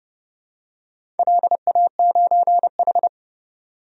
Morse code, heard as a keyed tone, about 31 wpm, 700 Hz. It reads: LA9H
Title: DXCC-100 Call Signs - 4 Characters - 30wpm